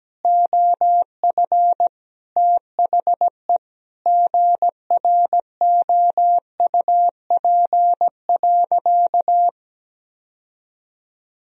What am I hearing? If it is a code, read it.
Morse code, heard as a keyed tone, about 17 wpm, 700 Hz. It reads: OF THE GROUP.
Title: Sentences from Top 1000 Words - Review Part 1 - 17wpm